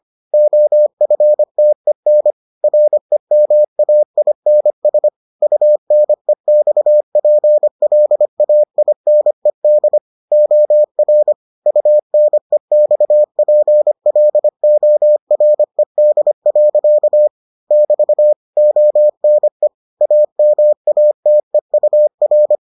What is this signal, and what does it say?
Morse code, heard as a keyed tone, about 25 wpm, 600 Hz. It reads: OFTEN REMAINS UNEXPLAINED OR UNEXPLORED. <BT> ONE AMATEUR